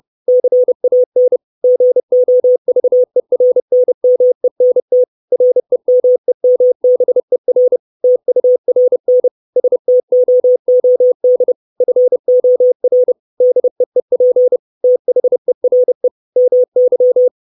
Morse code, heard as a keyed tone, about 30 wpm, 500 Hz. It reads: CAN GOVERNMENT REMEMBER TURN STOOD FOR DEEP THERE MY
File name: 632 Words 30WPM